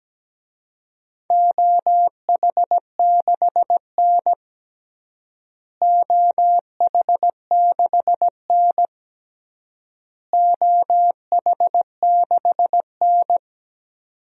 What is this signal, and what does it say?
Morse code, heard as a keyed tone, about 17 wpm, 700 Hz. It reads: OH6N OH6N OH6N
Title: DXCC-100 Call Signs - 4 Characters - Repeated 3x - 17wpm